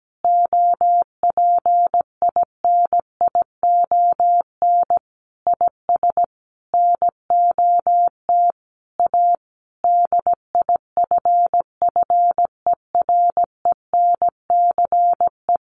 Morse code, heard as a keyed tone, about 17 wpm, 700 Hz. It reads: OPINION IS NOT A DIFFERENCE
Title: Sentences from Top 1400 Words 17wpm